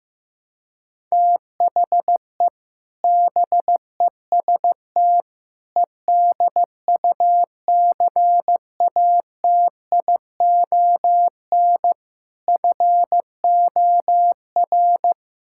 Morse code, heard as a keyed tone, about 15 wpm, 700 Hz. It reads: THE BEST EDUCATION FOR
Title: Sentences from Top 1000 Words - Review Part 6 - 15wpm